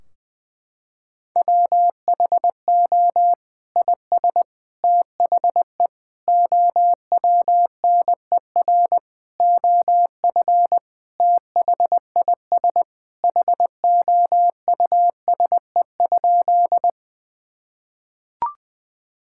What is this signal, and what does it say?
Morse code, heard as a keyed tone, about 19 wpm, 700 Hz. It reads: WHO IS THE OWNER OF THIS HOUSE?
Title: Sentences from Top 1600 Words 20wpm